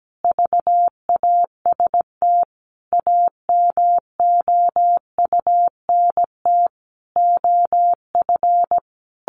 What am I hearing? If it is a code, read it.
Morse code, heard as a keyed tone, about 17 wpm, 700 Hz. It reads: VAST AMOUNT OF
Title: Sentences from Top 1300 Words 17wpm